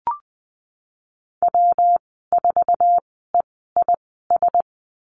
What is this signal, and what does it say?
Morse code, heard as a keyed tone, about 20 wpm, spaced as for 15 wpm, 700 Hz. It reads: W4EIS
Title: 128 Lesson 21g - Number 6 - Call Signs 20wpm